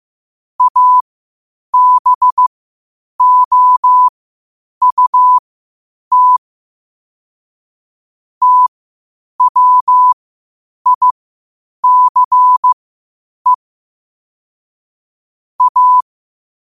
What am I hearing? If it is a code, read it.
Morse code, heard as a keyed tone, about 15 wpm, spaced as for 8 wpm, 1000 Hz. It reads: ABOUT TWICE A
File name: qotdfiles / 8wpm